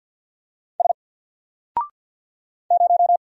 Morse code, heard as a keyed tone, about 50 wpm, 700 Hz. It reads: S 0
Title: Single Character (Letter:Number:Punctuation:Prosign) 50wpm